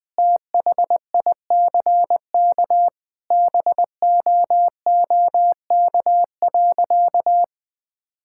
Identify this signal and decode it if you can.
Morse code, heard as a keyed tone, about 20 wpm, 700 Hz. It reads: THICK BOOK.
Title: Sentences with 4-Word Phrases - Spoken Beginning - 20wpm